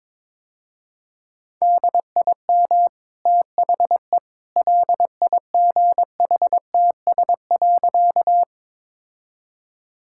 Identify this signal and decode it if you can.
Morse code, heard as a keyed tone, about 22 wpm, 690 Hz. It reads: DIM THE LIGHTS.
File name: practice / 22wpm / Easy Sentences with Hints - 22wpm